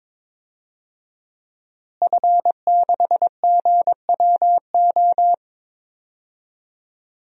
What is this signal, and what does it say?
Morse code, heard as a keyed tone, about 22 wpm, 700 Hz. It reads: F6GWO
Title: DXCC-100 Call Signs - 5 Characters - Repeated 3x - 22wpm